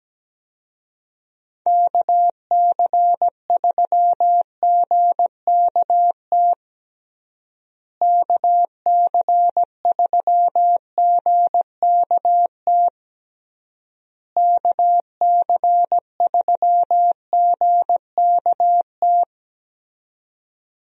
Morse code, heard as a keyed tone, about 17 wpm, 700 Hz. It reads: KC3GKT KC3GKT KC3GKT
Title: USA Call Signs - 6 Characters - Repeated 3x - 17wpm